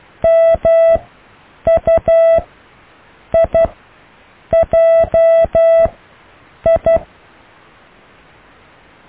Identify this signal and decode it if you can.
Morse code, heard as a keyed tone, about 12 wpm, 660 Hz. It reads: M U I J I
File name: cw / IJLMU